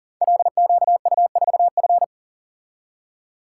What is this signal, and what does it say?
Morse code, heard as a keyed tone, about 42 wpm, 700 Hz. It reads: LQU4F
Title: Code Groups - 5 Letter:Numbers - 40wpm